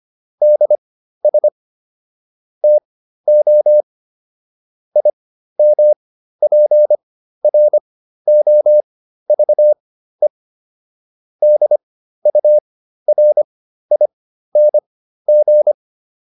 Morse code, heard as a keyed tone, about 25 wpm, spaced as for 13 wpm, 600 Hz. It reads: DS TO IMPROVE DURING